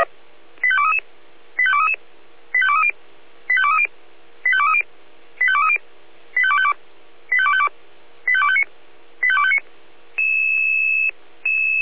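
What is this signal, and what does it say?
Five-tone selective call (ZVEI), heard as quick runs of five tones, tones of about 80 ms. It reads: 85233 85233 85230 85230 85230 85230 85252 85252 85259 85259
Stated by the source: ZVEI five-tone selective call, received on SDR.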